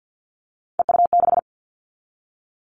Morse code, heard as a keyed tone, about 50 wpm, 700 Hz. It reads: EU6
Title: Callsign Prefixes for Top 50 DXCC - 50wpm